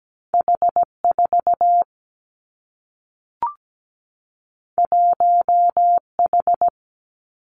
Morse code, heard as a keyed tone, about 17 wpm, 700 Hz. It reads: H4 1H
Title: Two Challenging Letter-Number Combinations 17wpm